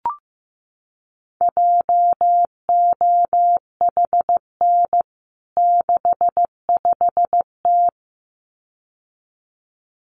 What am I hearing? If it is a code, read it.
Morse code, heard as a keyed tone, about 15 wpm, 700 Hz. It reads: JOHN 65T
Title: CWT Exchange 15wpm